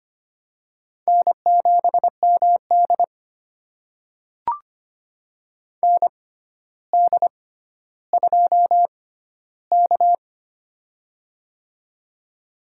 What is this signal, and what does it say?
Morse code, heard as a keyed tone, about 25 wpm, spaced as for 13 wpm, 700 Hz. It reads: N7MD ND2K
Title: USA Call Signs - 4 Characters - 6x Character Spacing - 25wpm